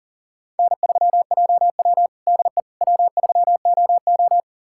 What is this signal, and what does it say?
Morse code, heard as a keyed tone, about 37 wpm, 700 Hz. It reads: N3JW DE W3OO
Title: USA Callsign DE Callsign - 4 Characters - 40wpm